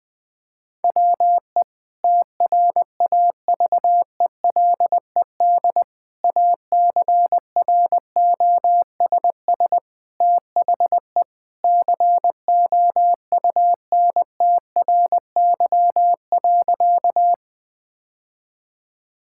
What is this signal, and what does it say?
Morse code, heard as a keyed tone, about 21 wpm, 700 Hz. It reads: WE TRAVELED ACROSS THE COUNTRY.
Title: Sentences with 2-Word Phrases - Spoken Beginning - 20wpm